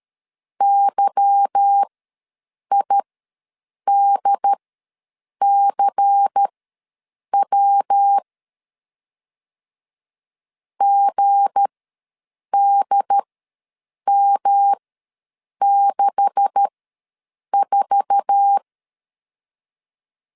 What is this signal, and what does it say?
Morse code, heard as a keyed tone, about 13 wpm, spaced as for 7 wpm, 800 Hz. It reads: YIDCW GDM64
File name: LEZIONI / 9 286 ws 12wpm 3351